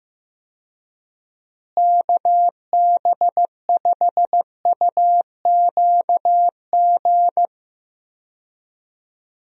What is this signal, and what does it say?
Morse code, heard as a keyed tone, about 15 wpm, 700 Hz. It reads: KB5UQG
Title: USA Call Signs - 6 Characters - Repeated 2x - 15wpm